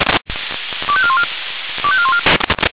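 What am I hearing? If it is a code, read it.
Five-tone selective call (ZVEI), heard as quick runs of five tones, tones of about 70 ms. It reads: 25414 25414
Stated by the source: ZVEI five-tone selective call received on an SDR.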